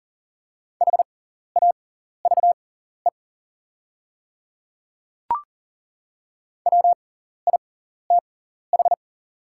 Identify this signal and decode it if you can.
Morse code, heard as a keyed tone, about 40 wpm, spaced as for 14 wpm, 690 Hz. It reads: HAVE WITH
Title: 6x Character Spacing - Top 100 Words - 40wpm